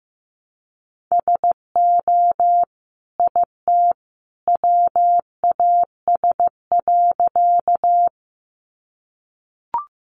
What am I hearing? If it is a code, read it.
Morse code, heard as a keyed tone, about 15 wpm, 700 Hz. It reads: SO IT WAS.
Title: Sentences from Top 100 Words 15wpm